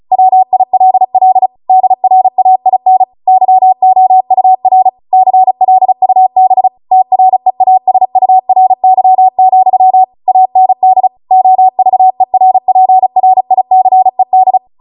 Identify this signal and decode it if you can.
Morse code, heard as a keyed tone, about 36 wpm, 750 Hz. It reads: WILL DRAIN YOUR CLUB TREASURY, AND OVERPRICED
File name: Morse / 35WPM